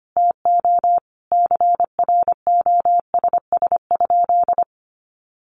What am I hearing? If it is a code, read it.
Morse code, heard as a keyed tone, about 25 wpm, 700 Hz. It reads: TO CROSS?
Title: Sentences from Top 300 Words 25wpm